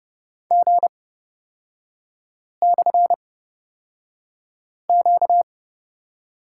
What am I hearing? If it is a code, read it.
Morse code, heard as a keyed tone, about 30 wpm, spaced as for 5 wpm, 700 Hz. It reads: G/Q
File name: Lesson 38e - Letter q - Three Characters 30wpm